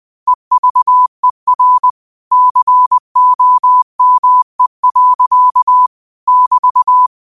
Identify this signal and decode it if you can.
Morse code, heard as a keyed tone, about 20 wpm, 1000 Hz. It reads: EVER COME. <BT>
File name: 20wpm